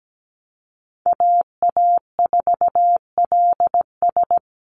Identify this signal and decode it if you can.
Morse code, heard as a keyed tone, about 17 wpm, 700 Hz. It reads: AA4LS
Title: Call Signs Repeated 2x - 17wpm